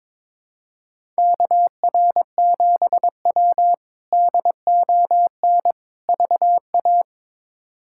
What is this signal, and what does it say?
Morse code, heard as a keyed tone, about 22 wpm, 700 Hz. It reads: KR7W DON VA
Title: K1USN SST QSO Practice - 22wpm